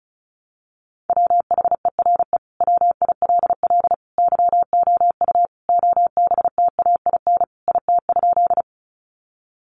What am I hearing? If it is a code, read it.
Morse code, heard as a keyed tone, about 34 wpm, 700 Hz. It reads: WHERE WILL YOU OBTAIN IT?
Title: Sentences from Top 1300 Words 35wpm